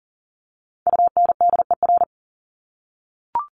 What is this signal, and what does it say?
Morse code, heard as a keyed tone, about 40 wpm, 700 Hz. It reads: UNDER
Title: Top 300 Words 40wpm